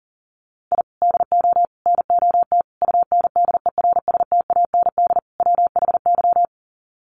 Morse code, heard as a keyed tone, about 40 wpm, 700 Hz. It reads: I DO NOT UNDERSTAND WHY
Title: Sets of 5 Words Encore 40wpm